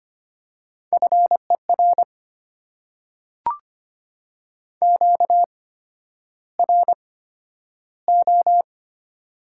Morse code, heard as a keyed tone, about 25 wpm, spaced as for 11 wpm, 700 Hz. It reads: FER QRO
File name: Top 90 QSO Elements - 8x Character Spacing - 25wpm